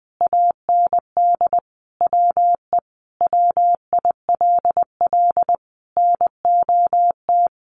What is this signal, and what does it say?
Morse code, heard as a keyed tone, about 19 wpm, 700 Hz. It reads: AND WE WILL NOT
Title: Sentences from Top 1100 Words 20wpm